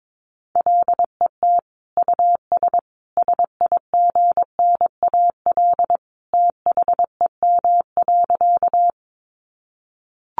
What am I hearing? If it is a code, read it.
Morse code, heard as a keyed tone, about 22 wpm, 700 Hz. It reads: LET US SIGNAL THEM.
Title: Sentences from Top 1400 Words 22wpm